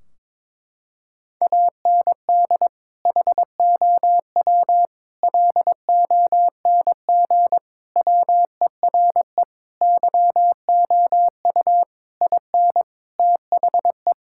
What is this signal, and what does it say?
Morse code, heard as a keyed tone, about 22 wpm, 700 Hz. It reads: AND HOW LONG WERE YOU IN THE
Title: Sentences from Top 1600 Words 22wpm